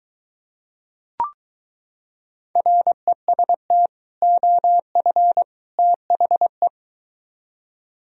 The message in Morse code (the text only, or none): REST OF THE